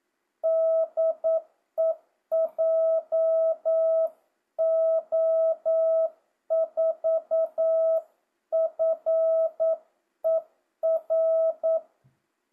Morse code, heard as a keyed tone, about 9 wpm, 650 Hz. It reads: DE JO4FER
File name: jh8cbh / 2025cw